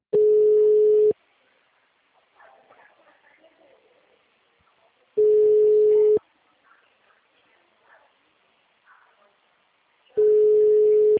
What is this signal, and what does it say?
Ringback tone (Europe, 425 Hz), with speech.